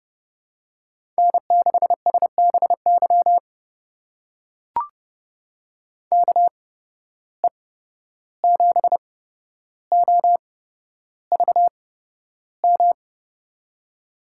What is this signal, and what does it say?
Morse code, heard as a keyed tone, about 30 wpm, spaced as for 12 wpm, 700 Hz. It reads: N6SBY KE7OVM
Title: USA Call Signs - 8x Character Spacing - 30wpm